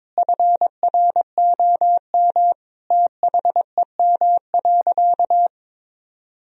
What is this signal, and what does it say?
Morse code, heard as a keyed tone, about 22 wpm, 700 Hz. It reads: FROM THEM.
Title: Sentences from Top 1000 Words - Review Part 4 - 22wpm